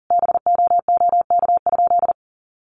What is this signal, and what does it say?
Morse code, heard as a keyed tone, about 40 wpm, 700 Hz. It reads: BOOK?
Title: Sentences from Top 1200 Words 40wpm